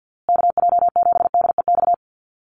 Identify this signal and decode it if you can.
Morse code, heard as a keyed tone, about 50 wpm, 700 Hz. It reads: KJ7DEX
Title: USA Call Signs Repeated 3x 50wpm